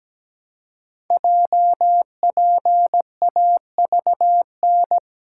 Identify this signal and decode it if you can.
Morse code, heard as a keyed tone, about 17 wpm, 700 Hz. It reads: JPAVN
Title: Code Groups - 5 Letters - 17wpm